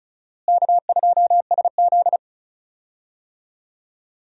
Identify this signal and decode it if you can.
Morse code, heard as a keyed tone, about 34 wpm, 700 Hz. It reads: K2SZ
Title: USA Call Signs - 4 Characters - Repeated 2x - 35wpm